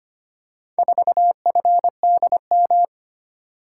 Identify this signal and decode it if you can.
Morse code, heard as a keyed tone, about 25 wpm, 700 Hz. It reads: 4FDM
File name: practice / 25wpm / Four Letter-Number Combinations - 25wpm